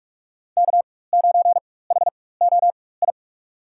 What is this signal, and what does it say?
Morse code, heard as a keyed tone, about 43 wpm, spaced as for 21 wpm, 700 Hz. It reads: K9HOI
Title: USA Call Signs - 5 Characters - 4x Character Spacing - 45wpm